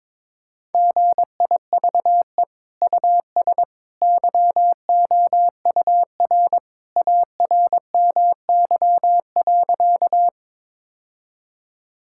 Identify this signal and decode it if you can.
Morse code, heard as a keyed tone, about 22 wpm, 700 Hz. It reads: GIVE US YOUR ARMY.
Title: Sentences from Top 1000 Words - Review Part 5 - 22wpm